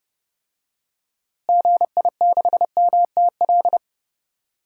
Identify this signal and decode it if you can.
Morse code, heard as a keyed tone, about 30 wpm, 700 Hz. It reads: GI6MTL
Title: DXCC-100 Call Signs - 6 Characters - Repeated 3x - 30wpm